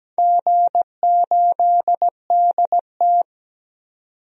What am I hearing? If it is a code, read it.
Morse code, heard as a keyed tone, about 17 wpm, 700 Hz. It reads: G8DT
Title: DXCC-100 Call Signs - 4 Characters - 17wpm